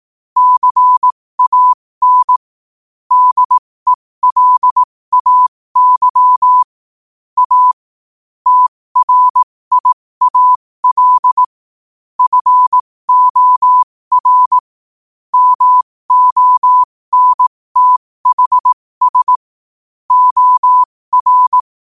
Morse code, heard as a keyed tone, about 18 wpm, spaced as for 15 wpm, 1000 Hz. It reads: CAN DELAY A TRIAL FOR MONTHS OR
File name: qotdfiles / 15wpm